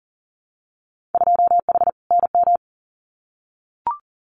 Morse code, heard as a keyed tone, about 41 wpm, 700 Hz. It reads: 2H NM
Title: Winter Field Day Exchanges Encore 40wpm